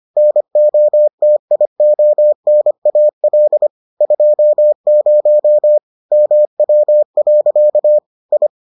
Morse code, heard as a keyed tone, about 25 wpm, 600 Hz. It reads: NOTIONAL 20 MW. I